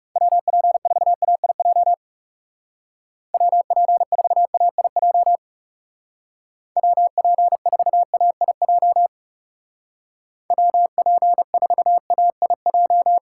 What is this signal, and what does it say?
Morse code, heard as a keyed tone, about 37 wpm, 700 Hz. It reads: WP4AIJ WP4AIJ WP4AIJ WP4AIJ
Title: Speed Racing - USA 2x3 Call Signs 30wpm